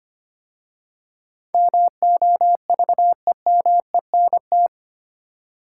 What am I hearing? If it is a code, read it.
Morse code, heard as a keyed tone, about 25 wpm, 700 Hz. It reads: MOVEMENT
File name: 8 Letter Words - First 3 Letters Given - 25wpm